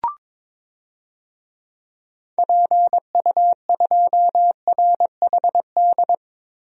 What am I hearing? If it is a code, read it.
Morse code, heard as a keyed tone, about 22 wpm, 700 Hz. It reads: PU2RHD
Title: DXCC-100 Call Signs - 6 Characters - 22wpm